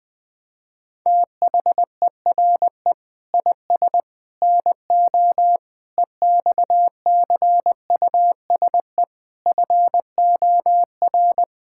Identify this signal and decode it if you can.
Morse code, heard as a keyed tone, about 21 wpm, 700 Hz. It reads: THERE IS NO EXCUSE FOR
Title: Sentences from Top 1000 Words - Review Part 6 - 20wpm